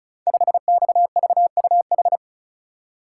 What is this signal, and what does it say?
Morse code, heard as a keyed tone, about 36 wpm, 700 Hz. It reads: 5XVUH
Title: Code Groups - 5 Letter:Numbers - 35wpm